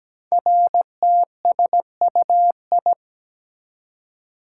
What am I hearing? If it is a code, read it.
Morse code, heard as a keyed tone, about 17 wpm, 700 Hz. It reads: RTSUI
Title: Code Groups - 5 Letters - 17wpm